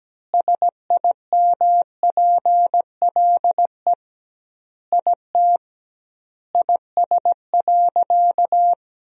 Morse code, heard as a keyed tone, about 17 wpm, 700 Hz. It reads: SIMPLE IT IS.
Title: 2x Word-Spacing - Sentences from Top 400 Words - 17wpm